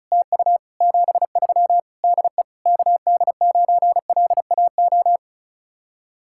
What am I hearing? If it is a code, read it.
Morse code, heard as a keyed tone, about 41 wpm, 700 Hz. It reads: TU 73 DE KD9LAO
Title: Speed-Racing Common QSO Phrases Encore - 35wpm